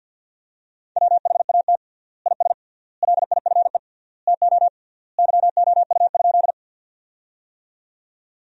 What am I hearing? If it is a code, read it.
Morse code, heard as a keyed tone, about 47 wpm, 700 Hz. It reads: WHAT IS LIFE TO YOU?
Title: 3x Word-Spacing - Sentences from Top 300 Words - 50wpm